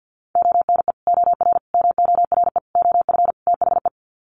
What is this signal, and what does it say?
Morse code, heard as a keyed tone, about 50 wpm, 700 Hz. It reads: ONE OR MORE OF THE
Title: Sets of 5 Words Encore 50wpm